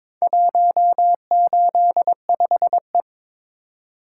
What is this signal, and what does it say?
Morse code, heard as a keyed tone, about 22 wpm, 700 Hz. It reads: 185E
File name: practice / 22wpm / Four Letter-Number Combinations - 22wpm